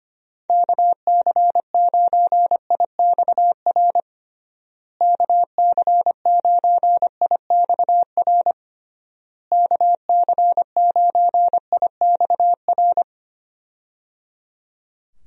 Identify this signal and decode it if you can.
Morse code, heard as a keyed tone, about 25 wpm, 700 Hz. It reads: KC9IXR KC9IXR KC9IXR
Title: USA Call Signs - 6 Characters - Repeated 3x - 25wpm